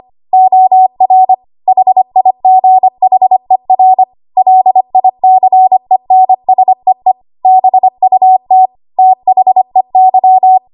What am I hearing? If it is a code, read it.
Morse code, heard as a keyed tone, about 25 wpm, 750 Hz. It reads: OR HIGHER LICENSEE BUT THEY
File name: Archive / 25 WPM / 25WPM